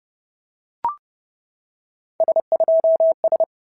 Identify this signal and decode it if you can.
Morse code, heard as a keyed tone, about 30 wpm, 650 Hz. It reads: S2S
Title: SOTA QSO Practice 30wpm